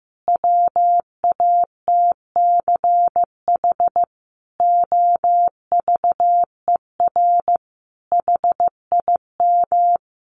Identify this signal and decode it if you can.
Morse code, heard as a keyed tone, about 15 wpm, 700 Hz. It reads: WATCH OVER HIM
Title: Sentences from Top 1500 Words 15wpm